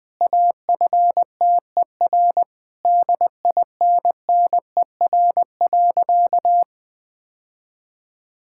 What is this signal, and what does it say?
Morse code, heard as a keyed tone, about 20 wpm, 700 Hz. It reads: AFTER DINNER.
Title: Difficult Sentences with Hints - 20wpm